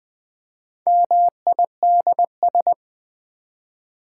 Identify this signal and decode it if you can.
Morse code, heard as a keyed tone, about 20 wpm, 700 Hz. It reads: MIDS
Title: Build A Word - 1 Letter At A Time - 5-Letter Words - 20wpm